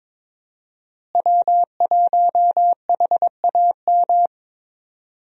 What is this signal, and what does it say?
Morse code, heard as a keyed tone, about 22 wpm, 700 Hz. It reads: W1HAM
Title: USA Call Signs - 5 Characters - Repeated 2x - 22wpm